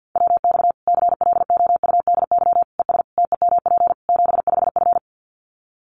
Morse code, heard as a keyed tone, about 50 wpm, 700 Hz. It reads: WX CLOUDY ES TEMP 75F
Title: Common QSO Phrases 50wpm